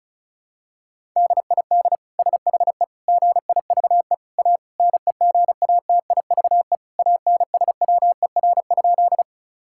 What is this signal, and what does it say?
Morse code, heard as a keyed tone, about 36 wpm, 700 Hz. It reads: DID SHE GIVE A NEGATIVE ANSWER?
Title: Sentences from Top 1000 Words - Review Part 6 - 35wpm